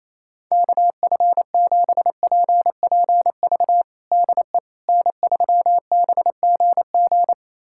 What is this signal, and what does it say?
Morse code, heard as a keyed tone, about 28 wpm, 700 Hz. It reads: KF7PPV DE N3BGG
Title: USA Callsign DE Callsign 28wpm